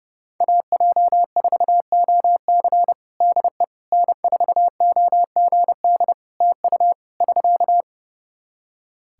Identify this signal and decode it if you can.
Morse code, heard as a keyed tone, about 31 wpm, 700 Hz. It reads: AJ4OC DE N4OGD TU <SK>
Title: Practice QSO - Exchange 4 - Ending - 30wpm